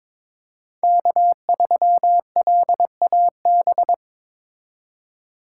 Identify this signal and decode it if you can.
Morse code, heard as a keyed tone, about 22 wpm, 700 Hz. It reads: K3LAB
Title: USA Call Signs - 5 Characters - 3x Character Spacing - 22wpm